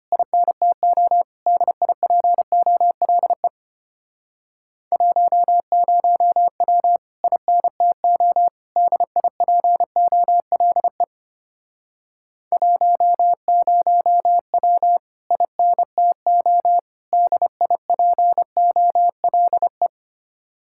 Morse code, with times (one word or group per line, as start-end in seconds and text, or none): 0.1-1.2 INTO
1.5-3.5 DIPOLE
4.9-7.0 10W
7.2-8.5 INTO
8.8-11.0 DIPOLE
12.5-15.0 10W
15.3-16.8 INTO
17.1-19.9 DIPOLE